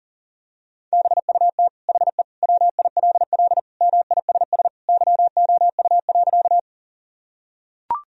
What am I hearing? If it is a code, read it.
Morse code, heard as a keyed tone, about 40 wpm, 700 Hz. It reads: BUT HE WILL MISS YOU.
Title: Sentences from Top 1000 Words - Review Part 3 - 40wpm